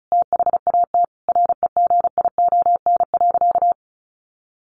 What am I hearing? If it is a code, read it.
Morse code, heard as a keyed tone, about 34 wpm, 700 Hz. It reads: THAT REGION.
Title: A Sound Of Thunder 35wpm